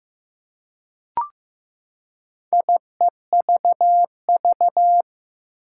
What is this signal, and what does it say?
Morse code, heard as a keyed tone, about 15 wpm, 700 Hz. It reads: IEVV
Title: Four Challenging Letter-Number Combinations 15wpm